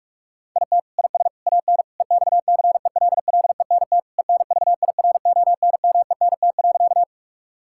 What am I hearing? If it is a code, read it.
Morse code, heard as a keyed tone, about 46 wpm, 690 Hz. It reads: IT IS AN EXCELLENT ENVIRONMENT.
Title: Sentences from Top 1000 Words - Review Part 5 - 45wpm